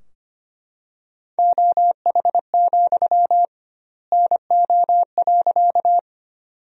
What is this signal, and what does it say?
Morse code, heard as a keyed tone, about 25 wpm, 700 Hz. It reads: OH, NO.
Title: 2x Word-Spacing - Sentences from Top 500 Words - 25wpm